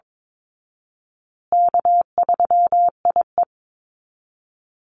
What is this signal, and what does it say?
Morse code, heard as a keyed tone, about 22 wpm, 690 Hz. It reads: K3IE
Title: Easy Call Signs Repeated 2x - 22wpm